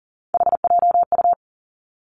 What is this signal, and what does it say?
Morse code, heard as a keyed tone, about 38 wpm, 700 Hz. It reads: HJU
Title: Three Challenging Letter-Number Combinations 40wpm